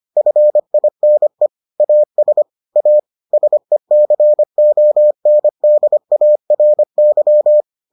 Morse code, heard as a keyed tone, about 25 wpm, 600 Hz. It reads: FINE AS A SECONDARY